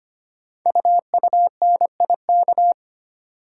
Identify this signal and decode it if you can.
Morse code, heard as a keyed tone, about 25 wpm, 700 Hz. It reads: UUNIK
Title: Code Groups - 5 Letters - 25wpm